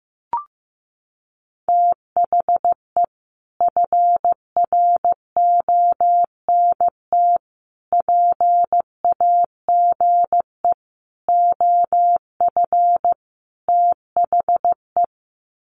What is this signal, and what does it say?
Morse code, heard as a keyed tone, about 15 wpm, 700 Hz. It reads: THE FRONT PAGE OF THE
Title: Sets of 5 Words Encore 15wpm